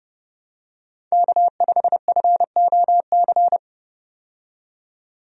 Morse code, heard as a keyed tone, about 30 wpm, 700 Hz. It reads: K5FOC
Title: USA Call Signs - 5 Characters - 30wpm